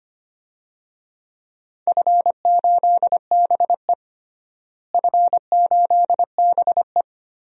Morse code, heard as a keyed tone, about 25 wpm, 700 Hz. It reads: F8BE F8BE
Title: DXCC-100 Call Signs - 4 Characters - Repeated 2x - 25wpm